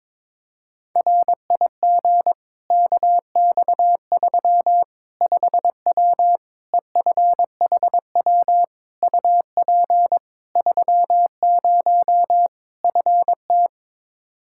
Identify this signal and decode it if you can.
Morse code, heard as a keyed tone, about 22 wpm, 700 Hz. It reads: RIG KX3 5W EFHW UP 30 FT
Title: Practice QSO - Exchange 2 - Info - 22wpm